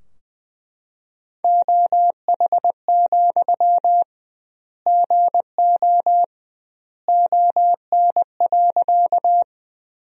Morse code, heard as a keyed tone, about 20 wpm, 700 Hz. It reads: OH, GO ON.
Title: 2x Word-Spacing - Sentences from Top 500 Words - 20wpm